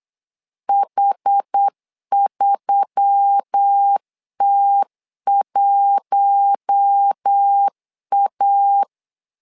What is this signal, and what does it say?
Morse code, heard as a keyed tone, about 8 wpm, 800 Hz. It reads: H3T1A
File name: LEZIONI / 5 LBVC / 5 L 8wpm 2831